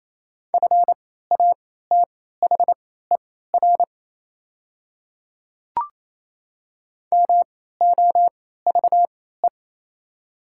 Morse code, heard as a keyed tone, about 28 wpm, spaced as for 16 wpm, 700 Hz. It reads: FATHER MOVE
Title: 3x Character Spacing - Top 200 Words - 28wpm